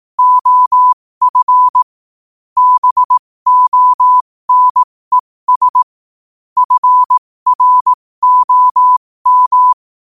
Morse code, heard as a keyed tone, about 18 wpm, spaced as for 15 wpm, 1000 Hz. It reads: OF BONES FROM
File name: qotdfiles / 15wpm